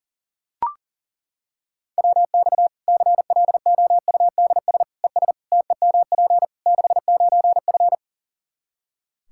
Morse code, heard as a keyed tone, about 40 wpm, 700 Hz. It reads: WX CLOUDS ES TEMP 69F
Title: K1USN SST QSO Practice 40wpm